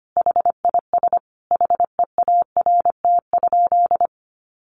Morse code, heard as a keyed tone, about 25 wpm, 700 Hz. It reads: HIS HEART?
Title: Sentences from Top 1200 Words 25wpm